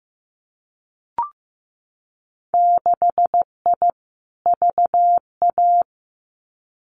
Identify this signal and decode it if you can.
Morse code, heard as a keyed tone, about 15 wpm, 700 Hz. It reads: 6I VA
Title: Winter Field Day Exchanges Encore 15wpm